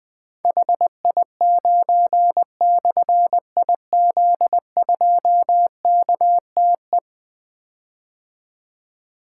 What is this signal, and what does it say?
Morse code, heard as a keyed tone, about 21 wpm, 700 Hz. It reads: HI9/IZ2KTE
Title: Difficult Call Signs 20wpm